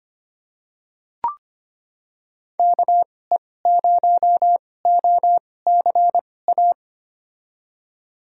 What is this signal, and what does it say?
Morse code, heard as a keyed tone, about 25 wpm, spaced as for 18 wpm, 700 Hz. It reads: KE0OCA
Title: USA Call Signs - 6 Characters - 2x Character Spacing - 25wpm